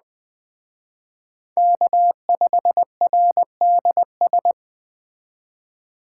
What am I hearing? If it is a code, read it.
Morse code, heard as a keyed tone, about 20 wpm, 700 Hz. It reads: K5RDS
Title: Speed Racing - USA 1x3 Call Signs 17wpm